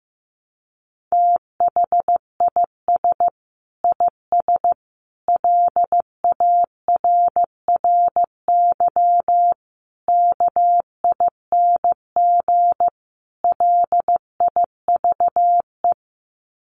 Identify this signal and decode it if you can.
Morse code, heard as a keyed tone, about 15 wpm, 700 Hz. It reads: THIS IS LARRY KING LIVE
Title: Sets of 5 Words Encore2 15wpm